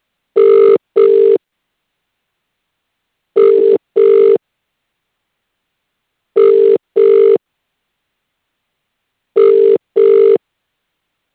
Ringback tone (UK), 400 + 450 Hz.